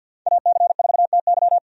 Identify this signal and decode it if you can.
Morse code, heard as a keyed tone, about 52 wpm, 700 Hz. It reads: AC4TY
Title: USA Call Signs - 5 Characters - 50wpm